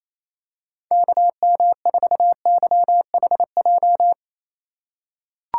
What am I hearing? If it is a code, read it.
Morse code, heard as a keyed tone, about 28 wpm, 700 Hz. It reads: KM4YHJ
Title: USA Call Signs - 2x Character Spacing - 28wpm